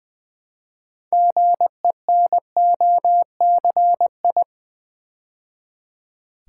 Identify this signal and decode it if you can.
Morse code, heard as a keyed tone, about 21 wpm, 700 Hz. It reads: GENOCI
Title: Build A Word - 2 Letters At A Time - 8-Letter Words - 20wpm